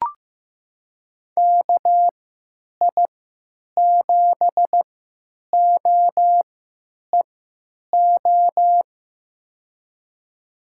Morse code, heard as a keyed tone, about 15 wpm, spaced as for 9 wpm, 700 Hz. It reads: KI7OEO
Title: USA Call Signs - 6 Characters - 3x Character Spacing - 15wpm